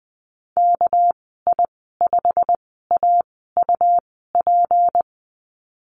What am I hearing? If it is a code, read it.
Morse code, heard as a keyed tone, about 20 wpm, spaced as for 15 wpm, 700 Hz. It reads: KI5AUP
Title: 238 Lesson 38g - Letter q - Call Signs 20wpm